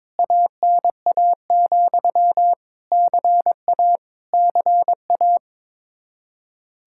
Morse code, heard as a keyed tone, about 22 wpm, 700 Hz. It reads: ANA, CA CA
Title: Practice QSO - Exchange 1 - Beginning - 22wpm